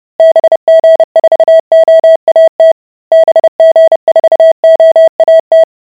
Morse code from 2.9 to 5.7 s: BG4OAT